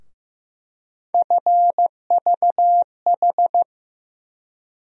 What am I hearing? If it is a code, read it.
Morse code, heard as a keyed tone, about 15 wpm, 700 Hz. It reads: FVH
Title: USA Call Sign Postfixes - 3 Letters 15wpm